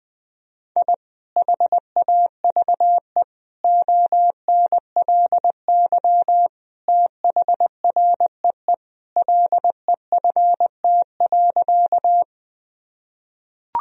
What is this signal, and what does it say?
Morse code, heard as a keyed tone, about 21 wpm, 700 Hz. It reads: I HAVE ONLY THREE LEFT.
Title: Sentences from Top 1000 Words - Review Part 1 - 20wpm